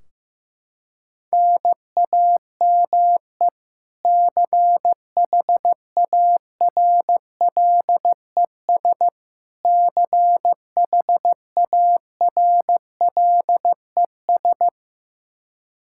Morse code, heard as a keyed tone, about 15 wpm, 700 Hz. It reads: NAME CHARLES CHARLES
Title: Practice QSO - Exchange 1 - Beginning - 15wpm